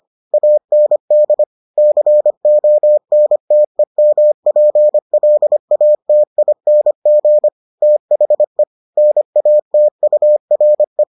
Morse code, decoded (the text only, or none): AND CONTEMPLATING THE NATURE